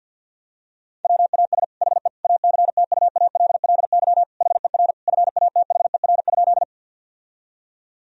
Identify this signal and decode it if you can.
Morse code, heard as a keyed tone, about 50 wpm, 700 Hz. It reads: WAS HE ACTUALLY HER FATHER?
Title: Sentences from Top 1000 Words - Review Part 4 - 50wpm